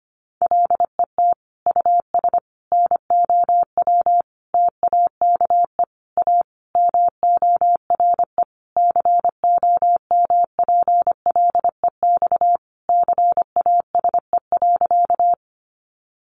Morse code, heard as a keyed tone, about 25 wpm, 700 Hz. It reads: LET US NOW TAKE A MORE COMPLEX CASE.
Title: Sentences from Top 900 Words 25wpm